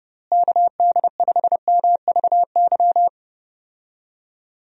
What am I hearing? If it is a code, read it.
Morse code, heard as a keyed tone, about 30 wpm, 700 Hz. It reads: KD5MVY
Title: USA Call Signs - 6 Characters - Repeated 2x - 30wpm